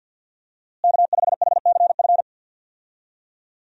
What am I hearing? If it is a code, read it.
Morse code, heard as a keyed tone, about 52 wpm, 700 Hz. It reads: K5HCF
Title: USA Call Signs - 5 Characters - Repeated 3x - 50wpm